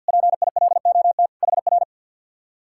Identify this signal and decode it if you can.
Morse code, heard as a keyed tone, about 66 wpm, spaced as for 58 wpm, 700 Hz. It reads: PILOT HR
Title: Speed-Racing Common QSO Phrases - 50wpm